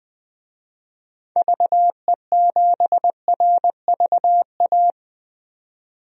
Morse code, heard as a keyed tone, about 20 wpm, 700 Hz. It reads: VE7RVA
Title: DXCC-100 Call Signs - 6 Characters - 20wpm